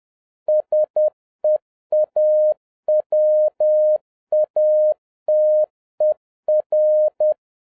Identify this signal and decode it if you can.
Morse code, heard as a keyed tone, about 10 wpm, 610 Hz. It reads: SEAWATER
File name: 2011-07-mp3 / n31 THE 3.11 2 10wpm